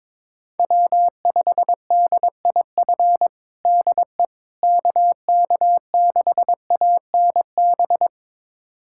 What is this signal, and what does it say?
Morse code, heard as a keyed tone, about 22 wpm, 700 Hz. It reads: W5DIF DE KK6ANB
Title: USA Callsign DE Callsign 22wpm